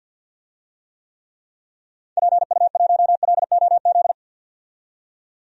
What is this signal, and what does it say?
Morse code, heard as a keyed tone, about 47 wpm, 700 Hz. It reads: PU1LOZ
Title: DXCC-100 Call Signs - 6 Characters - 50wpm